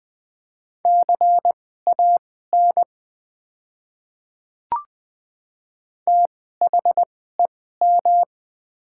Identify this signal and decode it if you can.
Morse code, heard as a keyed tone, about 21 wpm, spaced as for 15 wpm, 690 Hz. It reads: CAN THEM
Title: 2x Character Spacing - Top 100 Words - 20wpm